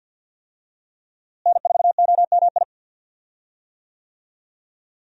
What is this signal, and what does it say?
Morse code, heard as a keyed tone, about 47 wpm, 700 Hz. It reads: N4OMI
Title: USA Call Signs - 5 Characters - Repeated 2x - 50wpm